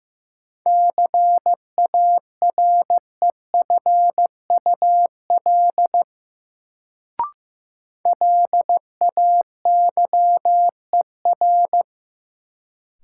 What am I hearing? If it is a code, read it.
Morse code, heard as a keyed tone, about 15 wpm, 700 Hz. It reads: CAREFUL LAYER
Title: Sentences from Top 1600 Words - New Word Drill - 15wpm